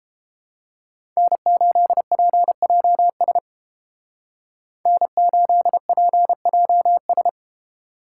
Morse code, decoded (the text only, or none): N8PJS N8PJS